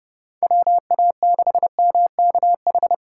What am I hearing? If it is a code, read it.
Morse code, heard as a keyed tone, about 31 wpm, 700 Hz. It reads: WA6MKH
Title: USA Call Signs - 6 Characters - 3x Character Spacing - 30wpm